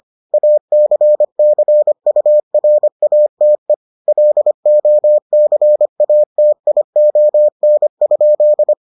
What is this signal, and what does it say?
Morse code, heard as a keyed tone, about 25 wpm, 600 Hz. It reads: ACCURATE LOCATION?